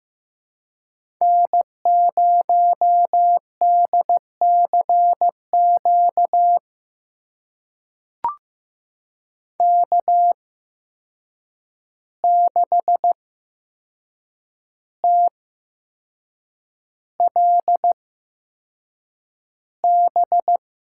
Morse code, heard as a keyed tone, about 15 wpm, spaced as for 6 wpm, 700 Hz. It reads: N0DCQ K6TLB
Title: USA Call Signs - 5 Characters - 8x Character Spacing - 15wpm